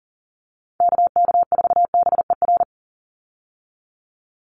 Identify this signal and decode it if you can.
Morse code, heard as a keyed tone, about 40 wpm, 700 Hz. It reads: KK4BER
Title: USA Call Signs 40wpm